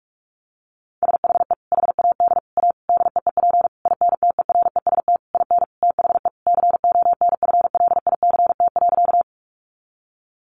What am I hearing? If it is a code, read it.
Morse code, heard as a keyed tone, about 44 wpm, 700 Hz. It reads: SHE HAD A DEEP INTEREST IN THE CONFLICT.
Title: Sentences from Top 1300 Words 45wpm